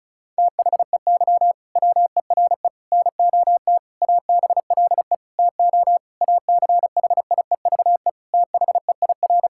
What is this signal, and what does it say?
Morse code, heard as a keyed tone, about 36 wpm, 700 Hz. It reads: THEY WERE NOT ABLE TO ACHIEVE THEIR
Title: Sentences from Top 1000 Words - Review Part 6 - 35wpm